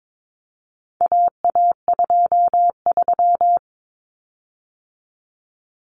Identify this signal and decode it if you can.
Morse code, heard as a keyed tone, about 22 wpm, 700 Hz. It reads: AA23
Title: Common Cut Numbers - 4-Digit 22wpm